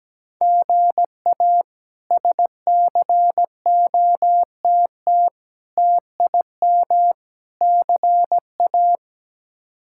Morse code, heard as a keyed tone, about 17 wpm, 700 Hz. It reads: GA SCOTT TIM CA
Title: K1USN SST QSO Practice - 17wpm